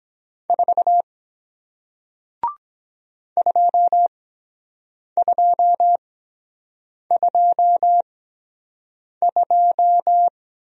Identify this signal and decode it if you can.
Morse code, heard as a keyed tone, about 23 wpm, spaced as for 7 wpm, 700 Hz. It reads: 4 2222